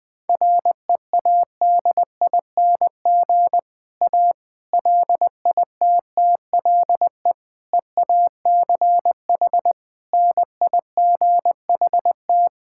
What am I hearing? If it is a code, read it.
Morse code, heard as a keyed tone, about 20 wpm, 700 Hz. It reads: READING A LITTLE EACH NIGHT
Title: Difficult Sentences with Hints - 20wpm